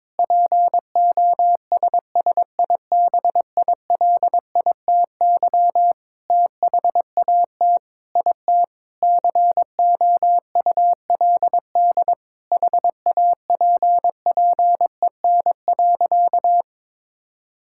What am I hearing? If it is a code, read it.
Morse code, heard as a keyed tone, about 22 wpm, 700 Hz. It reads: POSSIBILITY THAT IT COULD HAPPEN.
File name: Sentences with 9-Word Phrases - 22wpm